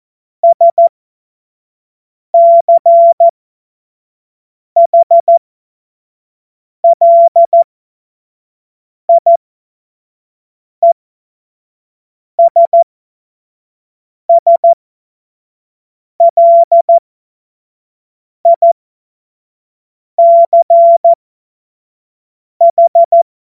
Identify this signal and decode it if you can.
Morse code, heard as a keyed tone, about 14 wpm, spaced as for 5 wpm, 680 Hz. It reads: SCHLIESSLICH
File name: Morsekurs / Witze / Witz 03. Januar 2023 25 F70